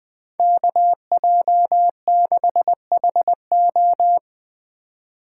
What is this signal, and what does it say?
Morse code, heard as a keyed tone, about 19 wpm, 700 Hz. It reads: KJ6HO
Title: Speed Racing - USA 2x2 Call Signs 17wpm